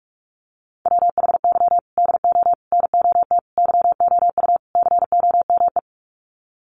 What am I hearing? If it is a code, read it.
Morse code, heard as a keyed tone, about 44 wpm, 700 Hz. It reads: WHY DO NOT YOU COME
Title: Sets of 5 Words Encore2 45wpm